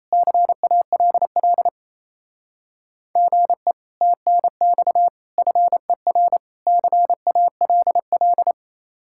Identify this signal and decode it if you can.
Morse code, heard as a keyed tone, about 34 wpm, 700 Hz. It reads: CALL GE TNX FER CALL
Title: Speed-Racing Common QSO Phrases Encore - 28wpm